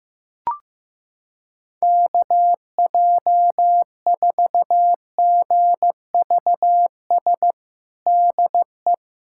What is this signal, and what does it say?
Morse code, heard as a keyed tone, about 16 wpm, 700 Hz. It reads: KJ4GVS DE
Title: USA Callsign DE Callsign - 6 Characters - 15wpm